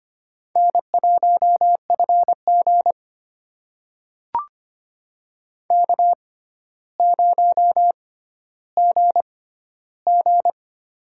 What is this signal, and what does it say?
Morse code, heard as a keyed tone, about 25 wpm, spaced as for 13 wpm, 700 Hz. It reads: N1FG K0GG
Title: USA Call Signs - 4 Characters - 6x Character Spacing - 25wpm